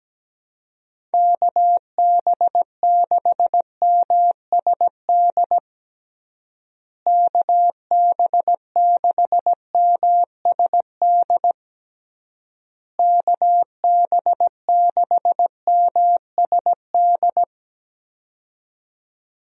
Morse code, heard as a keyed tone, about 17 wpm, 700 Hz. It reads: KB6MSD KB6MSD KB6MSD
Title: USA Call Signs - 6 Characters - Repeated 3x - 17wpm